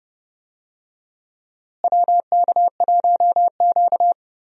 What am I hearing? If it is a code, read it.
Morse code, heard as a keyed tone, about 30 wpm, 700 Hz. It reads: WK1Q
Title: USA Call Signs - 4 Characters - Repeated 2x - 30wpm